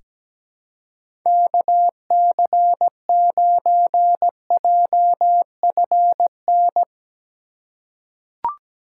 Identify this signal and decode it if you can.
Morse code, heard as a keyed tone, about 17 wpm, 700 Hz. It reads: KC9JFN
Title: USA Call Signs - 6 Characters - 4x Character Spacing - 17wpm